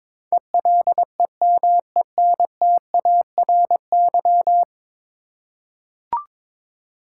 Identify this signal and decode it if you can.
Morse code, heard as a keyed tone, about 21 wpm, 700 Hz. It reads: ELEMENTARY
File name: practice / 22wpm / 10 Letter Words - First 3 Letters Given - 22wpm